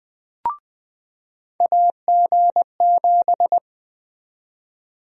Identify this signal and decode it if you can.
Morse code, heard as a keyed tone, about 20 wpm, 700 Hz. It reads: AG7
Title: USA Call Sign Prefixes - 20wpm